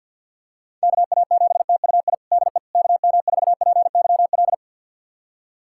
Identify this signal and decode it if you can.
Morse code, heard as a keyed tone, about 49 wpm, 700 Hz. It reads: KA7TUI DE KM4PYL
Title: USA Callsign DE Callsign 50wpm